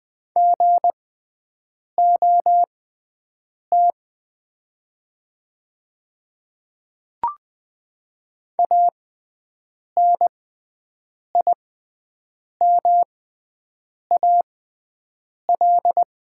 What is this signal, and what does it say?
Morse code, heard as a keyed tone, about 20 wpm, spaced as for 7 wpm, 700 Hz. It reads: GOT ANIMAL
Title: 6x Character Spacing - Top 300 Words - 20wpm